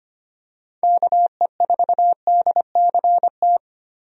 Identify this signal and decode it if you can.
Morse code, heard as a keyed tone, about 25 wpm, 700 Hz. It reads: KE4DCT
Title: USA Call Signs - 6 Characters - 25wpm